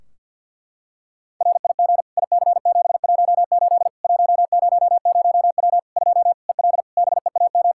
Morse code, heard as a keyed tone, about 49 wpm, 700 Hz. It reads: RIG IC718 100W 2 EL BEAM
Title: Practice QSO - Exchange 2 - Info - 50wpm